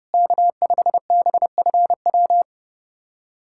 Morse code, heard as a keyed tone, about 30 wpm, 700 Hz. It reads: K5BFW
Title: USA Callsign DE Callsign 30wpm